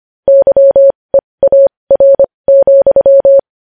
Morse code, heard as a keyed tone, about 25 wpm, spaced as for 20 wpm, 550 Hz. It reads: YEAR,
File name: podcast / News headlines-1x-25x20